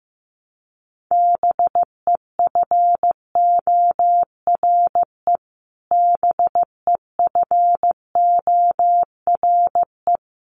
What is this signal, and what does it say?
Morse code, heard as a keyed tone, about 15 wpm, 700 Hz. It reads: BEFORE BEFORE
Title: Top 500 Words Repeated 15wpm